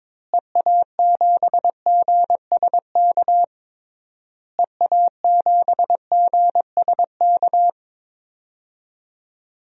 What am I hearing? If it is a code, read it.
Morse code, heard as a keyed tone, about 21 wpm, 700 Hz. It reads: EA7GSK EA7GSK
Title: DXCC-100 Call Signs - 6 Characters - Repeated 3x - 22wpm